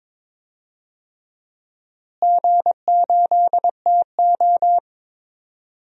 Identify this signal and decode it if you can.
Morse code, heard as a keyed tone, about 22 wpm, 700 Hz. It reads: G8TO
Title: DXCC-100 Call Signs - 4 Characters - Repeated 3x - 22wpm